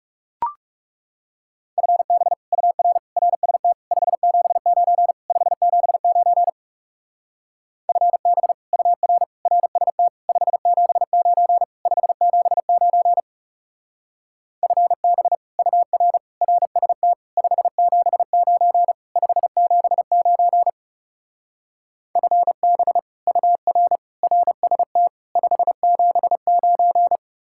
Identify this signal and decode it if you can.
Morse code, heard as a keyed tone, about 37 wpm, 700 Hz. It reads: FB UR RST 579 579 FB UR RST 579 579 FB UR RST 579 579 FB UR RST 579
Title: Speed-Racing Common QSO Phrases - 30wpm